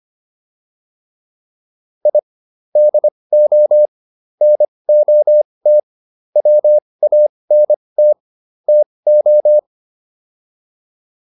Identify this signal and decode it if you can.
Morse code, heard as a keyed tone, about 25 wpm, spaced as for 20 wpm, 600 Hz. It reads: I DO NOT WANT TO